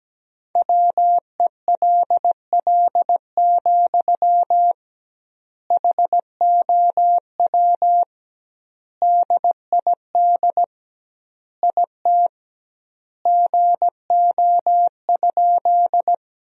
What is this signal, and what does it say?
Morse code, heard as a keyed tone, about 17 wpm, 700 Hz. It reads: WELL, HOW DID IT GO?
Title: 2x Word-Spacing - Sentences from Top 200 Words - 17wpm